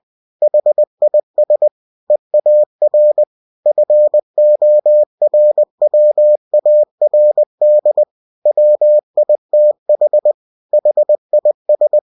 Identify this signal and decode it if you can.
Morse code, heard as a keyed tone, about 20 wpm, 600 Hz. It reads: HIS EAR FORWARD WITH HIS
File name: cw-books / winnie-the-pooh / winnie-20wpm-page-0054